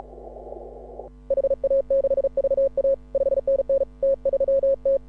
Morse code, heard as a keyed tone, about 36 wpm, 530 Hz. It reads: HA6VA 5NN T3T
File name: AUDIO2018 / HA6OI CW